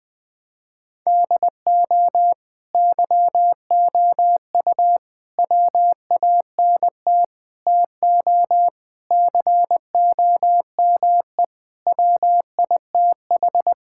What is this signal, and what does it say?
Morse code, heard as a keyed tone, about 21 wpm, 700 Hz. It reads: DO YOU WANT TO COME WITH
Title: Sentences with 3-Word Phrases - 20wpm